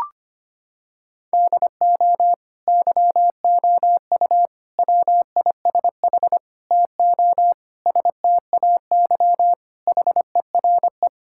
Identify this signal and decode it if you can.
Morse code, heard as a keyed tone, about 25 wpm, 700 Hz. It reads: DO YOU WISH TO STAY HERE
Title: Sentences from Top 1600 Words 25wpm